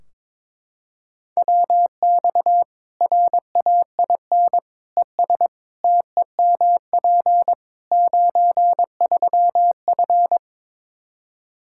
Morse code, heard as a keyed tone, about 21 wpm, 700 Hz. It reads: WX RAIN ES TEMP 93F
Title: Practice QSO - Exchange 2 - Info - 22wpm